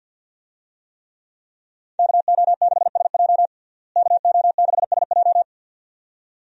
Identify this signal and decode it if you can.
Morse code, heard as a keyed tone, about 51 wpm, 700 Hz. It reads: KO6SJ KO6SJ
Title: USA Call Signs - 5 Characters - Repeated 2x - 50wpm